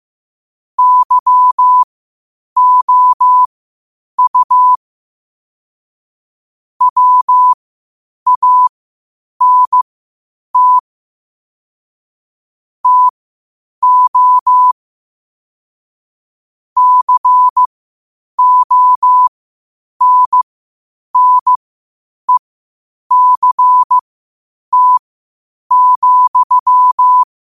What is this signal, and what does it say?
Morse code, heard as a keyed tone, about 15 wpm, spaced as for 8 wpm, 1000 Hz. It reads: YOU WANT TO CONNECT,
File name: qotdfiles / 8wpm